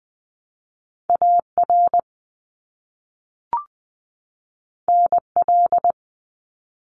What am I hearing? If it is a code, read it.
Morse code, heard as a keyed tone, about 20 wpm, 700 Hz. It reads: AR NL
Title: ARRL Sections 20wpm